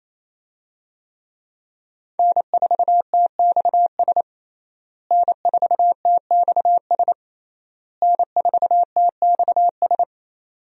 Morse code, heard as a keyed tone, about 28 wpm, 700 Hz. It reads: N4TXS N4TXS N4TXS
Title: USA Call Signs - 5 Characters - Repeated 3x - 28wpm